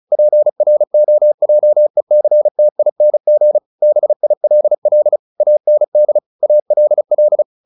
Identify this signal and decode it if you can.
Morse code, heard as a keyed tone, about 34 wpm, 600 Hz. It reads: PROJECTING BILL AND ALL
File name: the old man and the sea 2 35wpm